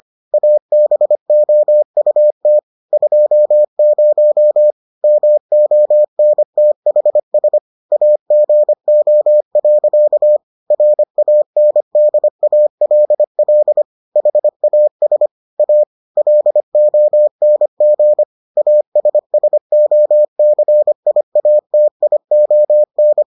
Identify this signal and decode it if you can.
Morse code, heard as a keyed tone, about 25 wpm, 600 Hz. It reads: ABOUT 20 MONTHS AGO. RANDALL HAS A LONG ASSOCIATION